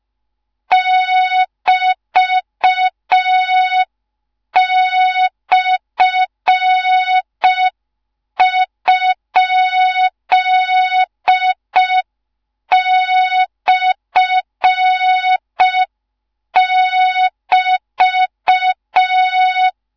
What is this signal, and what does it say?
Morse code, heard as a keyed tone, about 5 wpm, 3700 Hz. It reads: <BT>/?/<BT>
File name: Morze-HA2MN / hang / 13IRASJELEK1 HA2MN